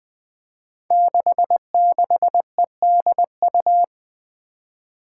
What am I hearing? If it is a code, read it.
Morse code, heard as a keyed tone, about 20 wpm, 700 Hz. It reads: 66EDU
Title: Five Challenging Letter-Number Combinations 20wpm